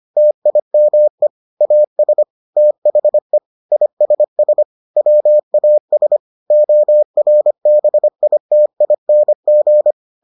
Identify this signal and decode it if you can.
Morse code, heard as a keyed tone, about 25 wpm, 600 Hz. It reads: TIME AS THE ISS WAS ORBITING